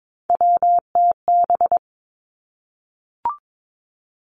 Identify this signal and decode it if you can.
Morse code, heard as a keyed tone, about 22 wpm, 700 Hz. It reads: WTB
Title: Three Challenging Letter-Number Combinations 22wpm